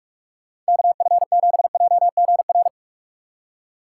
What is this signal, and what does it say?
Morse code, heard as a keyed tone, about 59 wpm, 700 Hz. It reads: KF7JGR